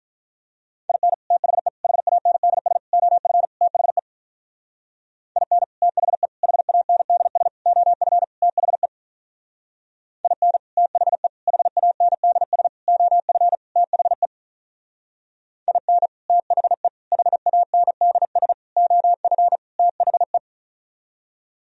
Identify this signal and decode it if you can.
Morse code, heard as a keyed tone, about 46 wpm, 700 Hz. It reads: IN THE HANDS OF THE IN THE HANDS OF THE IN THE HANDS OF THE IN THE HANDS OF THE